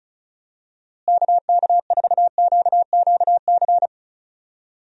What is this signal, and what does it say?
Morse code, heard as a keyed tone, about 34 wpm, 700 Hz. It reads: KK4QQC
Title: USA Call Signs - 6 Characters - Repeated 3x - 35wpm